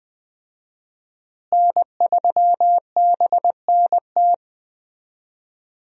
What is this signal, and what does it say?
Morse code, heard as a keyed tone, about 20 wpm, 700 Hz. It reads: N3BNT
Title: USA Call Signs - 5 Characters - Repeated 2x - 20wpm